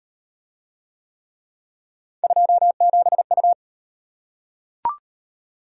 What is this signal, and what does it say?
Morse code, heard as a keyed tone, about 33 wpm, 700 Hz. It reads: 27U